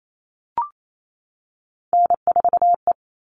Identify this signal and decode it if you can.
Morse code, heard as a keyed tone, about 28 wpm, 700 Hz. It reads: N4E
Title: All Cut Numbers - 3-Digit 28wpm